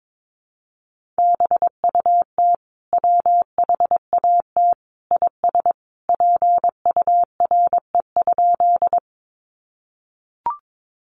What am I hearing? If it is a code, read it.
Morse code, heard as a keyed tone, about 22 wpm, 700 Hz. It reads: BUT WHAT IS PURE?
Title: Sentences from Top 1500 Words 22wpm